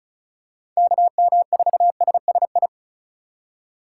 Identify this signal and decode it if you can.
Morse code, heard as a keyed tone, about 36 wpm, 700 Hz. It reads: KM4SSI
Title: USA Call Signs - 6 Characters - 3x Character Spacing - 35wpm